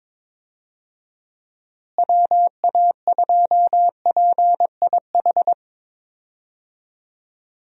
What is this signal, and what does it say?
Morse code, heard as a keyed tone, about 22 wpm, 700 Hz. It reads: WA2PIH
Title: USA Call Signs - 6 Characters - Repeated 3x - 22wpm